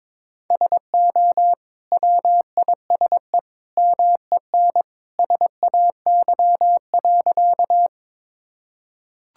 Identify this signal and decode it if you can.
Morse code, heard as a keyed tone, about 22 wpm, 700 Hz. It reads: SO WISE MEN SAY.
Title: Sentences from Top 1600 Words 22wpm